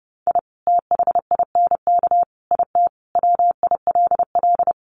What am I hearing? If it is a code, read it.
Morse code, heard as a keyed tone, about 30 wpm, 700 Hz. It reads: I THINK IT WILL
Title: Sets of 5 Words Encore2 30wpm